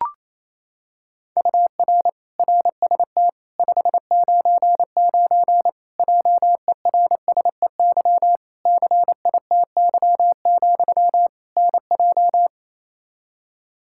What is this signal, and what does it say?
Morse code, heard as a keyed tone, about 28 wpm, 700 Hz. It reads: UR RST 599 JERSEY CITY, NJ
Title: Practice QSO - Exchange 1 - Beginning - 28wpm